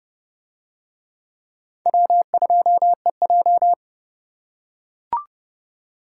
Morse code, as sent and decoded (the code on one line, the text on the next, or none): .-- ..--- . .---
W2EJ